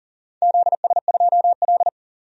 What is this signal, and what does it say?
Morse code, heard as a keyed tone, about 42 wpm, 700 Hz. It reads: ZS2L
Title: DXCC-100 Call Signs - 4 Characters - 40wpm